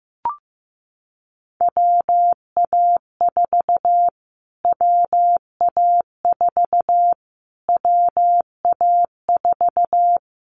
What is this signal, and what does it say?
Morse code, heard as a keyed tone, about 15 wpm, 700 Hz. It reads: WA4 WA4 WA4
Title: Callsign Prefixes for Top 25 DXCC - Repeated 3x - 15wpm